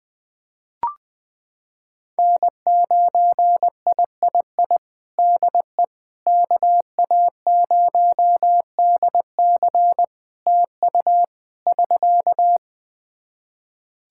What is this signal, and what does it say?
Morse code, heard as a keyed tone, about 20 wpm, 700 Hz. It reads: N9III DE KA0DC TU <SK>
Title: Practice QSO - Exchange 4 - Ending - 20wpm